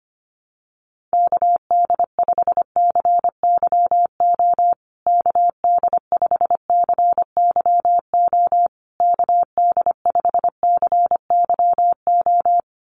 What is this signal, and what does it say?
Morse code, heard as a keyed tone, about 25 wpm, 700 Hz. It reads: KD5CYO KD5CYO KD5CYO
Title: USA Call Signs Repeated 3x 25wpm